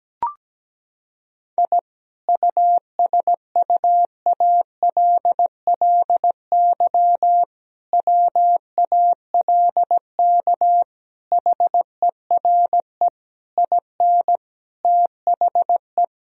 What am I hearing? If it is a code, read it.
Morse code, heard as a keyed tone, about 17 wpm, 700 Hz. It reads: I USUALLY WALK HERE IN THE
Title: Sentences from Top 1000 Words - Review Part 3 - 17wpm